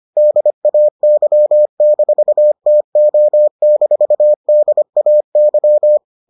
Morse code, heard as a keyed tone, about 25 wpm, 600 Hz. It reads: DAY-TO-DAY